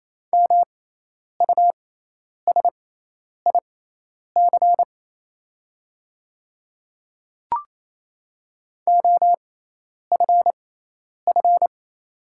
Morse code, heard as a keyed tone, about 28 wpm, spaced as for 10 wpm, 700 Hz. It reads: MUSIC OFF
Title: 6x Character Spacing - Top 300 Words - 28wpm